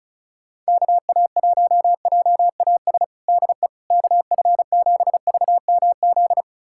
Morse code, heard as a keyed tone, about 36 wpm, 700 Hz. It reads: KA1JAS DE KF7VMZ
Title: USA Callsign DE Callsign - 6 Characters - 35wpm